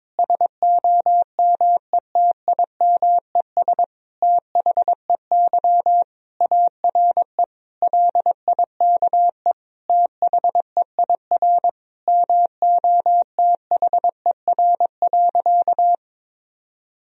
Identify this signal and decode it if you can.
Morse code, heard as a keyed tone, about 22 wpm, 700 Hz. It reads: SOMETIMES THEY ARE LIKE THEIR MOTHER.
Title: Sentences from Top 1000 Words - Review Part 3 - 22wpm